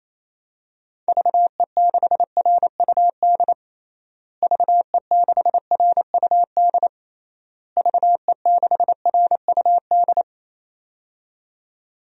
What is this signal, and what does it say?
Morse code, heard as a keyed tone, about 28 wpm, 700 Hz. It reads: VE6RUD VE6RUD VE6RUD
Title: DXCC-100 Call Signs - 6 Characters - Repeated 3x - 28wpm